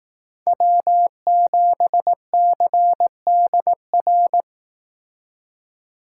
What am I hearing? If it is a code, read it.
Morse code, heard as a keyed tone, about 18 wpm, 700 Hz. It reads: W7CDR
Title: Speed Racing - USA Call Signs 15wpm